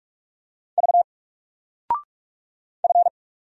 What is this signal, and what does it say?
Morse code, heard as a keyed tone, about 45 wpm, 700 Hz. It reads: V F
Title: Single Letter-Number 45wpm